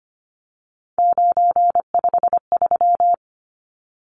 Morse code, heard as a keyed tone, about 25 wpm, 700 Hz. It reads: 953
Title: Numbers - 3-Digit 25wpm